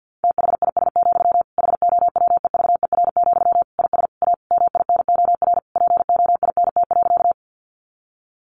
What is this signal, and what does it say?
Morse code, heard as a keyed tone, about 50 wpm, 700 Hz. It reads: THIS, HOWEVER, IS A MINOR POINT.
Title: Sentences from Top 1500 Words 50wpm